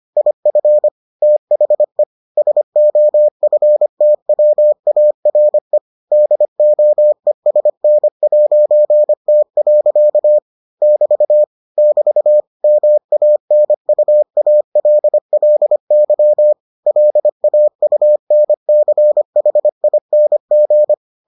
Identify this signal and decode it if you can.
Morse code, heard as a keyed tone, about 25 wpm, 600 Hz. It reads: IF THE SOFTWARE DOESN'T. <BT> <BT> MANUALLY LAUNCHING